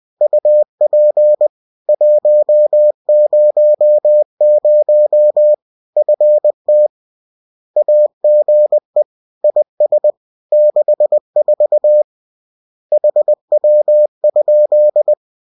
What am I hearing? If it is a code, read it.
Morse code, heard as a keyed tone, about 20 wpm, 600 Hz. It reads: UP 100 FT AGE IS 64 HW?